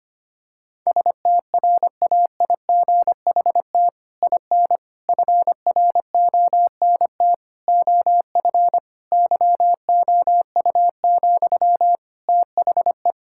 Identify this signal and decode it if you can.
Morse code, heard as a keyed tone, about 25 wpm, 700 Hz. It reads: STRAIGHT IN FRONT OF YOU, THE
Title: Sentences from Top 1000 Words - Review Part 6 - 25wpm